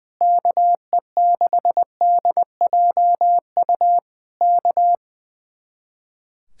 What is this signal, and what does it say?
Morse code, heard as a keyed tone, about 20 wpm, 700 Hz. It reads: KE6DJU K
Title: Speed-Racing Common QSO Phrases Encore - 20wpm